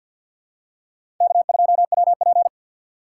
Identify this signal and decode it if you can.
Morse code, heard as a keyed tone, about 50 wpm, 700 Hz. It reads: K2WP
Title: USA Call Signs - 4 Characters - 3x Character Spacing - 50wpm